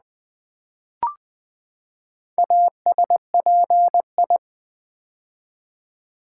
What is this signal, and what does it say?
Morse code, heard as a keyed tone, about 20 wpm, 700 Hz. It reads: ASPI
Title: Build A Word - 4 Letters At A Time - 10-Letter Words - 20wpm